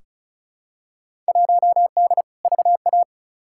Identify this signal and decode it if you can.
Morse code, heard as a keyed tone, about 34 wpm, 700 Hz. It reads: 1D VA
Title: Field Day Exchanges - 35wpm